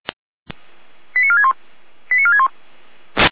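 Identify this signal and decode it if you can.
Five-tone selective call (ZVEI), heard as quick runs of five tones, tones of about 80 ms. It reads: 89461 89461
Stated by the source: ZVEI five-tone call received by SDR.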